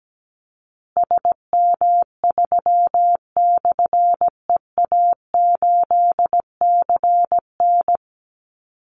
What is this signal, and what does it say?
Morse code, heard as a keyed tone, about 17 wpm, 700 Hz. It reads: SM3/EA8CN
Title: Call Signs Repeated 3x - 17wpm